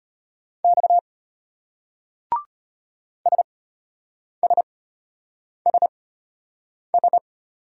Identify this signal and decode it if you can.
Morse code, heard as a keyed tone, about 32 wpm, spaced as for 8 wpm, 700 Hz. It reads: X SSSS